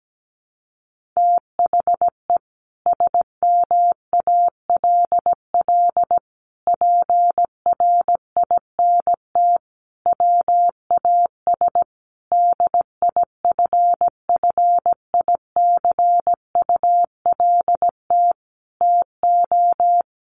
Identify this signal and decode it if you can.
Morse code, heard as a keyed tone, about 17 wpm, 700 Hz. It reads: THE SMALL PRINT WAS DIFFICULT TO
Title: Sentences from Top 1400 Words 17wpm